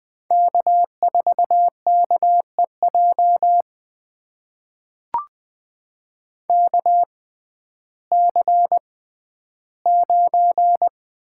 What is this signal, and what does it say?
Morse code, heard as a keyed tone, about 20 wpm, spaced as for 12 wpm, 700 Hz. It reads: K4KEJ KC9
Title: USA Call Signs - 6x Character Spacing - 20wpm